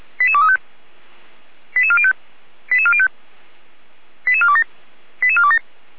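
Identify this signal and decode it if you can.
Five-tone selective call (ZVEI), heard as quick runs of five tones, tones of about 70 ms. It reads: 88235 88405 88405 88427 88427
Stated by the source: ZVEI five-tone selective call received on SDR, linked to the Feuerwehr Orlamünde fire brigade.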